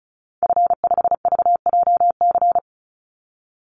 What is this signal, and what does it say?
Morse code, heard as a keyed tone, about 34 wpm, 700 Hz. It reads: F5VJC
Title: Call Signs Repeated 2x - 35wpm